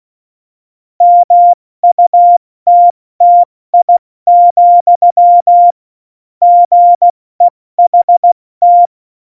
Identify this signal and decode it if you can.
Morse code, heard as a keyed tone, about 16 wpm, 700 Hz. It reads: MUTTI, GEHT
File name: Morsekurs / Witze / Witz 10. Oktober 17, Tempo 70